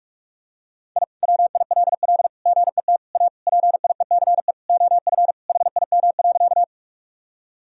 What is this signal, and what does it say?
Morse code, heard as a keyed tone, about 41 wpm, 700 Hz. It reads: I WILL GET A PIECE OF HIM.
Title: Sentences from Top 1000 Words - Review Part 2 - 45wpm